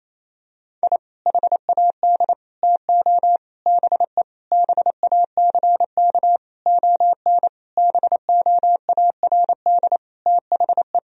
Morse code, heard as a keyed tone, about 28 wpm, 700 Hz. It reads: I HAD TO BE BACK ON BOARD THE
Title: Sentences from Top 1000 Words - Review Part 4 - 28wpm